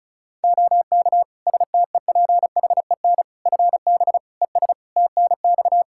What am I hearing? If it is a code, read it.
Morse code, heard as a keyed tone, about 34 wpm, 700 Hz. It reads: OK STEPHEN FB ES TNX
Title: Practice QSO - Exchange 2 - Info - 35wpm